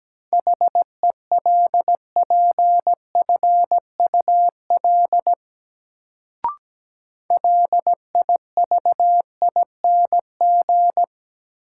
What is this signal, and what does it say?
Morse code, heard as a keyed tone, about 17 wpm, 700 Hz. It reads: HELPFUL LIVING
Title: Sentences from Top 1000 Words - Review Part 6 - New Word Drill - 17wpm